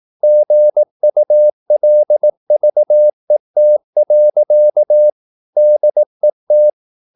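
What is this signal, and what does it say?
Morse code, heard as a keyed tone, about 18 wpm, 600 Hz. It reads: GULVET. DET
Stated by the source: Slow (QRS) Morse practice at 18 wpm, CW med Gnister.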